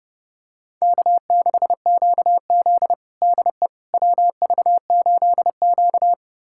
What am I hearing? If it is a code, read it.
Morse code, heard as a keyed tone, about 30 wpm, 700 Hz. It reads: K6QZ DE WV8Q
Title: USA Callsign DE Callsign - 4 Characters - 30wpm